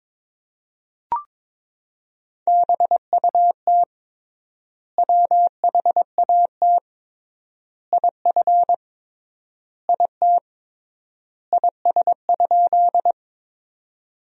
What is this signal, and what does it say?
Morse code, heard as a keyed tone, about 21 wpm, 700 Hz. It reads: BUT WHAT IF IT IS?
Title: 3x Word-Spacing - Sentences from Top 100 Words - 22wpm